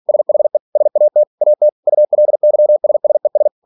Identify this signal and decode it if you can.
Morse code, heard as a keyed tone, about 46 wpm, 600 Hz. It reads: SHE SAT AT ULYSSES
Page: Never 47wpm